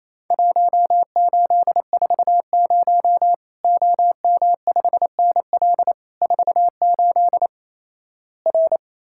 Morse code, heard as a keyed tone, about 28 wpm, 700 Hz. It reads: 1840 OM5NL 48 R
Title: Worked All Europe DX Contest - QTC Exchange - 28wpm